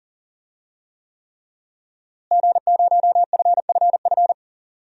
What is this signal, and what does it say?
Morse code, heard as a keyed tone, about 39 wpm, 700 Hz. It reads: G0FFF
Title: DXCC-100 Call Signs - 5 Characters - Repeated 3x - 40wpm